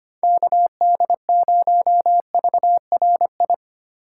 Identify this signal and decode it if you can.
Morse code, heard as a keyed tone, about 25 wpm, 700 Hz. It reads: KD0VRI
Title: USA Call Signs - 6 Characters - 4x Character Spacing - 25wpm